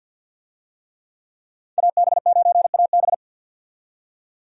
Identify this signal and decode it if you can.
Morse code, heard as a keyed tone, about 50 wpm, 700 Hz. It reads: AB9AB
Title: USA Call Signs - 5 Characters - 50wpm